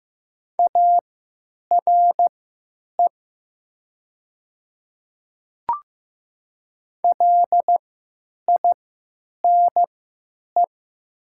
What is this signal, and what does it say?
Morse code, heard as a keyed tone, about 15 wpm, spaced as for 9 wpm, 700 Hz. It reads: ARE LINE
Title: 3x Character Spacing - Top 300 Words - 15wpm